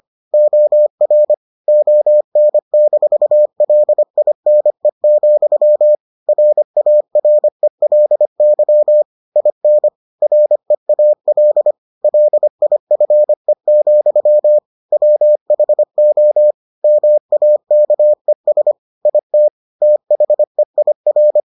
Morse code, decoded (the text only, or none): OR ON-LINE, RARELY IN REAL LIFE, WHO MAKES IT THEIR